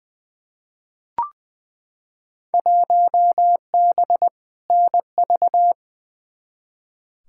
Morse code, heard as a keyed tone, about 20 wpm, 700 Hz. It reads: 1B NV
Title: Field Day Exchanges - 20wpm